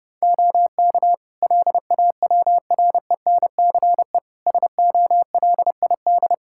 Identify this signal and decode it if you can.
Morse code, heard as a keyed tone, about 30 wpm, 700 Hz. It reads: OK LAWRENCE SOLID
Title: Practice QSO - Exchange 3 - More Info - 30wpm